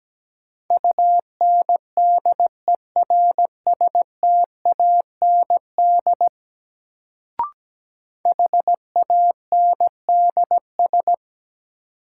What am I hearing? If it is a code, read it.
Morse code, heard as a keyed tone, about 17 wpm, 700 Hz. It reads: UNDERSTAND HANDS
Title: Sets of 4 Words - New Word Drill - 17wpm